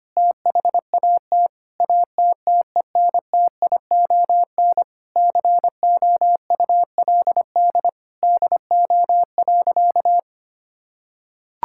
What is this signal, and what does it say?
Morse code, heard as a keyed tone, about 25 wpm, 700 Hz. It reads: THAT ATTENTION COULD DO.
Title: Sentences from Top 1000 Words - Review Part 6 - 25wpm